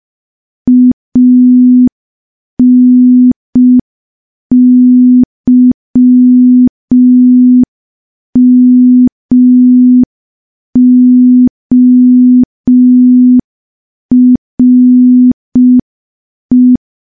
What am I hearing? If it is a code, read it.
Morse code, heard as a keyed tone, about 5 wpm, 250 Hz. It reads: ANYMORE
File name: morse / morse-training-05WPM